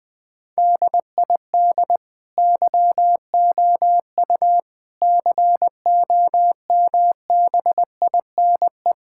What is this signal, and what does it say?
Morse code, heard as a keyed tone, about 20 wpm, 700 Hz. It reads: DID YOU COMBINE
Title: Sentences from Top 1600 Words 20wpm